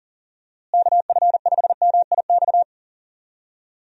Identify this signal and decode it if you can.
Morse code, heard as a keyed tone, about 40 wpm, 700 Hz. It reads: KF5MIX
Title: USA Call Signs - 6 Characters - Repeated 3x - 40wpm